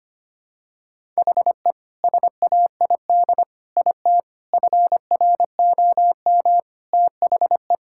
Morse code, heard as a keyed tone, about 25 wpm, 700 Hz. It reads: HE SAID IT FROM THE
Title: Sentences from Top 1600 Words 25wpm